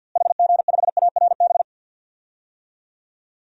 Morse code, heard as a keyed tone, about 49 wpm, 700 Hz. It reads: HG5ARB
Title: DXCC-100 Call Signs - 6 Characters - Repeated 2x - 50wpm